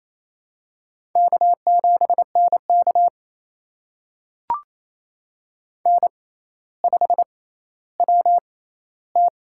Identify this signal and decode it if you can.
Morse code, heard as a keyed tone, about 28 wpm, spaced as for 14 wpm, 700 Hz. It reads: K7NK N5WT
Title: USA Call Signs - 4 Characters - 6x Character Spacing - 28wpm